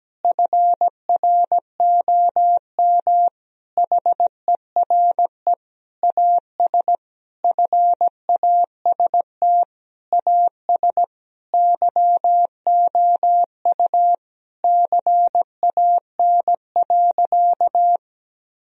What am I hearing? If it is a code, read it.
Morse code, heard as a keyed tone, about 17 wpm, 700 Hz. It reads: FROM HERE AS FAST AS YOU CAN.
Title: Sentences from Top 1000 Words - Review Part 2 - 17wpm